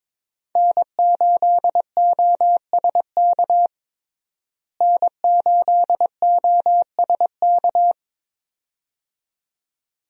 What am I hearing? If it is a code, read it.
Morse code, heard as a keyed tone, about 22 wpm, 700 Hz. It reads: N8OSK N8OSK
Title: USA Call Signs - 5 Characters - Repeated 2x - 22wpm